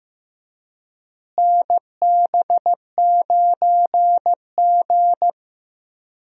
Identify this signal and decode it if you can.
Morse code, heard as a keyed tone, about 15 wpm, 700 Hz. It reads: NB9G
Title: USA Call Signs - 4 Characters - Repeated 3x - 15wpm